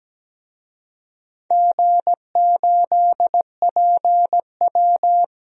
Morse code, heard as a keyed tone, about 17 wpm, 700 Hz. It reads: G8PW
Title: DXCC-100 Call Signs - 4 Characters - Repeated 2x - 17wpm